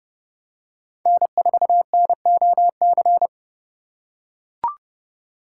Morse code, heard as a keyed tone, about 30 wpm, 700 Hz. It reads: N4NOC
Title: USA Call Signs - 5 Characters - 2x Character Spacing - 30wpm